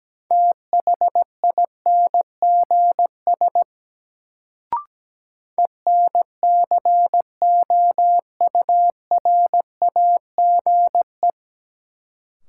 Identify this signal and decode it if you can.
Morse code, heard as a keyed tone, about 17 wpm, 700 Hz. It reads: THINGS ENCOURAGE
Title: Sentences from Top 1400 Words - New Word Drill Alt - 17wpm